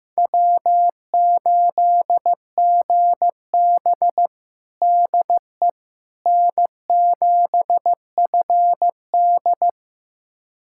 Morse code, heard as a keyed tone, about 15 wpm, 700 Hz. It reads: W8GB DE N7FD
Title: USA Callsign DE Callsign - 4 Characters - 15wpm